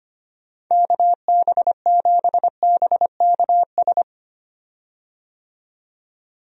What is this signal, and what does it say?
Morse code, heard as a keyed tone, about 25 wpm, 700 Hz. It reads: KB7BKS
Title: USA Call Signs - 6 Characters - 25wpm